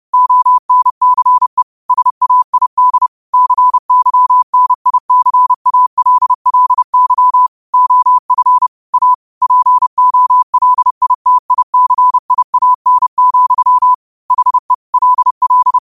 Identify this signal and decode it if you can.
Morse code, heard as a keyed tone, about 30 wpm, 1000 Hz. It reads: ONCE SAID CYNICALLY OF A POLITICIAN, HE LL
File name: 30wpm